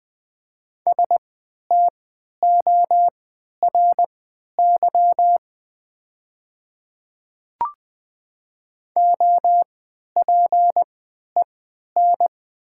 Morse code, heard as a keyed tone, about 20 wpm, spaced as for 11 wpm, 700 Hz. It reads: STORY OPEN
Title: 3x Character Spacing - Top 300 Words - 20wpm